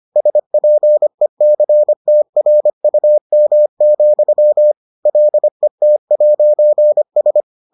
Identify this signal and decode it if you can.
Morse code, heard as a keyed tone, about 25 wpm, 600 Hz. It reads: SPECTRUM, LET'S